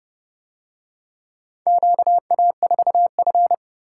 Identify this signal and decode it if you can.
Morse code, heard as a keyed tone, about 29 wpm, 700 Hz. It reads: QA4F